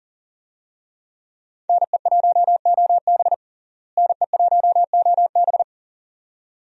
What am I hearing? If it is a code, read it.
Morse code, heard as a keyed tone, about 40 wpm, 700 Hz. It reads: NE1OB NE1OB
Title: USA Call Signs - 5 Characters - Repeated 2x - 40wpm